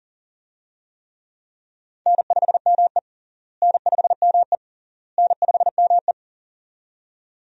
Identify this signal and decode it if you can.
Morse code, heard as a keyed tone, about 37 wpm, 700 Hz. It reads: N5ME N5ME N5ME
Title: USA Call Signs - 4 Characters - Repeated 3x - 40wpm